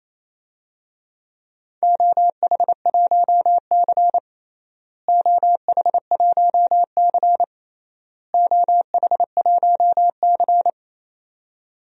Morse code, heard as a keyed tone, about 28 wpm, 700 Hz. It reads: OH1C OH1C OH1C
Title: DXCC-100 Call Signs - 4 Characters - Repeated 3x - 28wpm